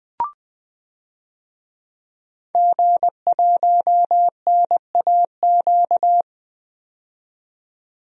Morse code, heard as a keyed tone, about 20 wpm, 700 Hz. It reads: G1NAQ
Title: DXCC-100 Call Signs - 5 Characters - 20wpm